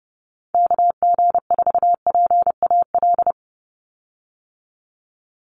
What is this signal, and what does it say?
Morse code, heard as a keyed tone, about 30 wpm, 700 Hz. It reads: KG4PAL
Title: USA Call Signs 30wpm